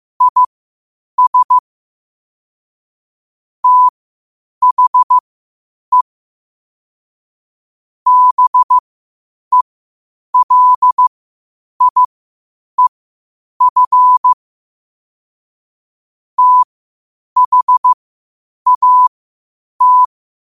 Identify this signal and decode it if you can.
Morse code, heard as a keyed tone, about 15 wpm, spaced as for 8 wpm, 1000 Hz. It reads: IS THE BELIEF THAT
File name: qotdfiles / 8wpm